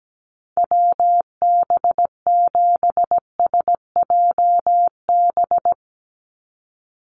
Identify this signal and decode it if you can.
Morse code, heard as a keyed tone, about 17 wpm, 700 Hz. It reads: WB7SJB
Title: USA Call Signs 17wpm